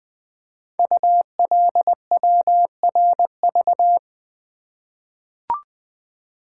Morse code, heard as a keyed tone, about 20 wpm, 700 Hz. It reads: ULWRV
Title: Code Groups - 5 Letters - 20wpm